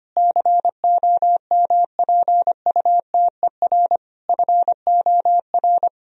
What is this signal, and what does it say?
Morse code, heard as a keyed tone, about 25 wpm, 700 Hz. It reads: COMPUTER FOR
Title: Sentences from Top 1000 Words - Review Part 3 - 25wpm